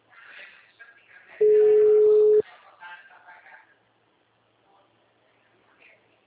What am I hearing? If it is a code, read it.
Ringback tone (Europe, 425 Hz).